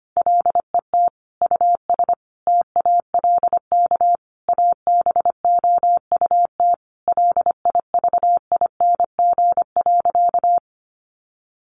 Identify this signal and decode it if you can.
Morse code, heard as a keyed tone, about 25 wpm, 700 Hz. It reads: LET US TALK ABOUT LIVING.
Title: Sentences from Top 1100 Words 25wpm